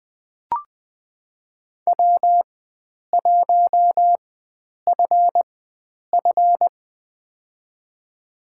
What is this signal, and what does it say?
Morse code, heard as a keyed tone, about 20 wpm, spaced as for 9 wpm, 700 Hz. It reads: W1FF
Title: USA Call Signs - 4 Characters - 4x Character Spacing - 20wpm